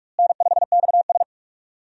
Morse code, heard as a keyed tone, about 46 wpm, 700 Hz. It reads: N5XS
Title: USA Call Signs - 4 Characters - Repeated 2x - 45wpm